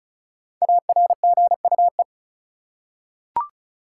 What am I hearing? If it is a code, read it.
Morse code, heard as a keyed tone, about 36 wpm, 700 Hz. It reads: ARGUE
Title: Five Letter Words - 35wpm